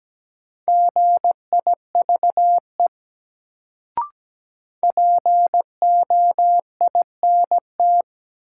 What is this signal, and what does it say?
Morse code, heard as a keyed tone, about 17 wpm, 700 Hz. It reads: GIVE POINT
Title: Sentences from Top 200 Words - New Word Drill - 17wpm